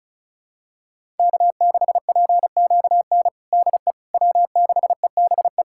Morse code, heard as a keyed tone, about 36 wpm, 700 Hz. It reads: K6PQN DE W6EBE
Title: USA Callsign DE Callsign - 5 Characters - 35wpm